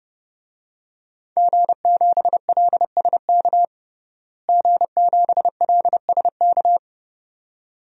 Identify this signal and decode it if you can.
Morse code, heard as a keyed tone, about 32 wpm, 700 Hz. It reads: G7LSK G7LSK
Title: DXCC-100 Call Signs - 5 Characters - Repeated 2x - 30wpm